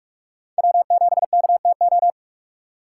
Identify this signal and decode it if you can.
Morse code, heard as a keyed tone, about 45 wpm, 700 Hz. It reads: W7KTO
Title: USA Call Signs - 3x Character Spacing - 45wpm